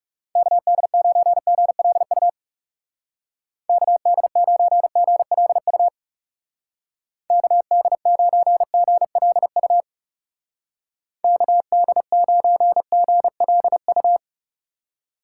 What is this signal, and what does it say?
Morse code, heard as a keyed tone, about 39 wpm, 700 Hz. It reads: KD9GLU KD9GLU KD9GLU KD9GLU